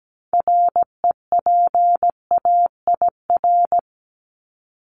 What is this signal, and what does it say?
Morse code, heard as a keyed tone, about 17 wpm, 700 Hz. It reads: REPAIR
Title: Top 1501-1750 Words 17wpm